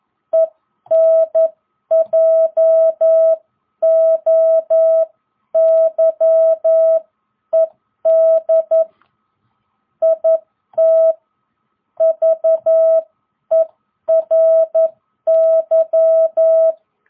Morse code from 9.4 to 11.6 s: IT